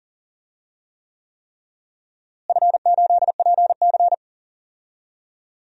Morse code, heard as a keyed tone, about 40 wpm, 700 Hz. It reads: F8PC
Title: DXCC-100 Call Signs - 4 Characters - Repeated 3x - 40wpm